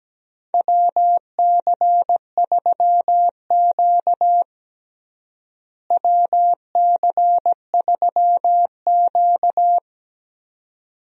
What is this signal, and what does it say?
Morse code, heard as a keyed tone, about 17 wpm, 700 Hz. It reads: WC3Q WC3Q
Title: USA Call Signs - 4 Characters - Repeated 2x - 17wpm